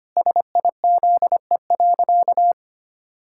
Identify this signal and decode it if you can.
Morse code, heard as a keyed tone, about 25 wpm, 700 Hz. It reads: SIZE.
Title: Pangram Challenge v2 - 25wpm